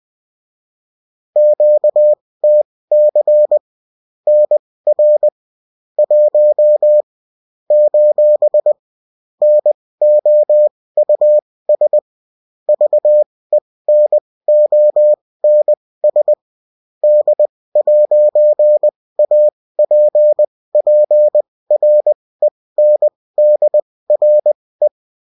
Morse code, heard as a keyed tone, about 20 wpm, spaced as for 16 wpm, 600 Hz. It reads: QTC NR 1 : NOUS VENONS D'APPRENDRE
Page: bulletin-uft-sous-forme-texte-et-en-cw